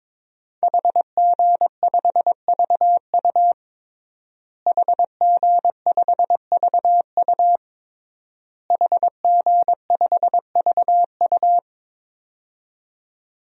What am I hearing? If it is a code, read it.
Morse code, heard as a keyed tone, about 22 wpm, 700 Hz. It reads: HG5VU HG5VU HG5VU
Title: DXCC-100 Call Signs - 5 Characters - Repeated 3x - 22wpm